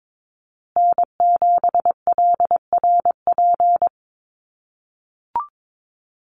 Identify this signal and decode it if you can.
Morse code, heard as a keyed tone, about 22 wpm, 700 Hz. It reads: N7LRP
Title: USA Call Signs Repeated 2x 22wpm